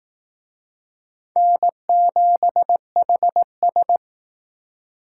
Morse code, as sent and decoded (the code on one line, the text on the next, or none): -. --... .... ...
N7HS